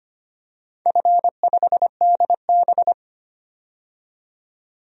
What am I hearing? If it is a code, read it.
Morse code, heard as a keyed tone, about 25 wpm, 700 Hz. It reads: F5DB
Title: DXCC-100 Call Signs - 4 Characters - 25wpm